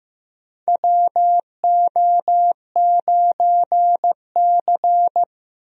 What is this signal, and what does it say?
Morse code, heard as a keyed tone, about 15 wpm, 700 Hz. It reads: WO9C
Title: USA Call Signs - 4 Characters - Repeated 3x - 15wpm